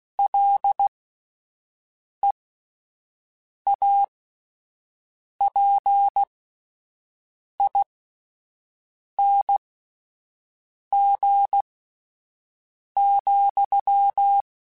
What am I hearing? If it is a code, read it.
Morse code, heard as a keyed tone, about 16 wpm, spaced as for 5 wpm, 800 Hz. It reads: LEAPING,